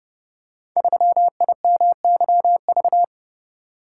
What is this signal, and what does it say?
Morse code, heard as a keyed tone, about 30 wpm, 700 Hz. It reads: 3IMYV
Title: Code Groups - 5 Letter:Numbers - 30wpm